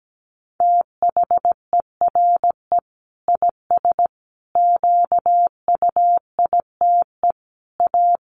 Morse code, heard as a keyed tone, about 17 wpm, 700 Hz. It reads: THERE IS QUITE A
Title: Sentences from Top 800 Words 17wpm